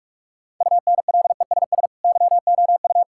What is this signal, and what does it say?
Morse code, heard as a keyed tone, about 41 wpm, 700 Hz. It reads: UNLESS YOU
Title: Sentences from Top 1000 Words - Review Part 6 - 45wpm